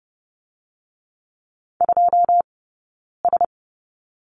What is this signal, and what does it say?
Morse code, heard as a keyed tone, about 30 wpm, 700 Hz. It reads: 2 S
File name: Lesson 16d - Letter c - Two Characters 30wpm